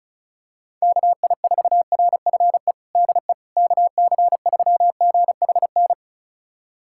Morse code, heard as a keyed tone, about 36 wpm, 700 Hz. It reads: KI4RFE DE KC3GHN
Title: USA Callsign DE Callsign - 6 Characters - 35wpm